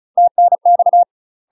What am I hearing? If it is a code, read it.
Morse code, heard as a keyed tone, about 33 wpm, 700 Hz. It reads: TNX
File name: UK MASTER QSOs TRAINING / 35WPM / UK MASTER QSOs 0017